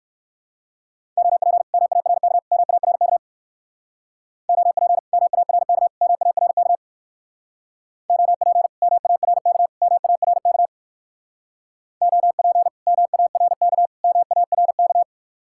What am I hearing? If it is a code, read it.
Morse code, heard as a keyed tone, about 59 wpm, 700 Hz. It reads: OP MARK MARK OP MARK MARK OP MARK MARK OP MARK MARK